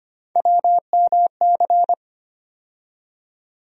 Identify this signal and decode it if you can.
Morse code, heard as a keyed tone, about 25 wpm, 700 Hz. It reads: WMC
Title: USA Call Sign Postfixes - 3 Letters 25wpm